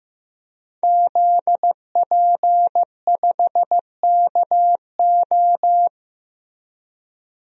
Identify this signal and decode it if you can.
Morse code, heard as a keyed tone, about 15 wpm, 700 Hz. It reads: ZP5KO
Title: DXCC-100 Call Signs - 5 Characters - Repeated 3x - 15wpm